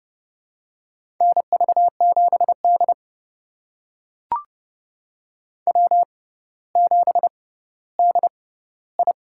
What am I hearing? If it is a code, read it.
Morse code, heard as a keyed tone, about 31 wpm, spaced as for 16 wpm, 700 Hz. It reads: NV7D W7DI
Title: USA Call Signs - 4 Characters - 6x Character Spacing - 30wpm